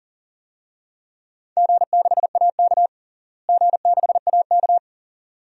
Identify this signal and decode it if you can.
Morse code, heard as a keyed tone, about 40 wpm, 700 Hz. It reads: G6AK G6AK
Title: DXCC-100 Call Signs - 4 Characters - Repeated 2x - 40wpm